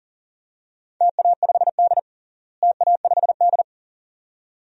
Morse code, heard as a keyed tone, about 39 wpm, 700 Hz. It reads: TA5D TA5D
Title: DXCC-100 Call Signs - 4 Characters - Repeated 2x - 40wpm